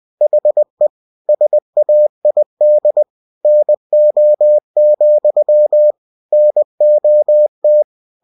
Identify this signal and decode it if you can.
Morse code, heard as a keyed tone, about 20 wpm, 600 Hz. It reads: HE SAID NO, NOT
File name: cw-books / winnie-the-pooh / winnie-20wpm-page-0031